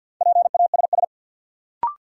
Morse code, heard as a keyed tone, about 48 wpm, 700 Hz. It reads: PASS
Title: Sentences from Top 400 Words - New Word Drill - 50wpm